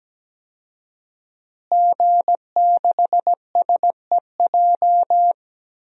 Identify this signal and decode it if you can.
Morse code, heard as a keyed tone, about 17 wpm, 700 Hz. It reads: G6SEJ
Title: DXCC-100 Call Signs - 5 Characters - Repeated 2x - 17wpm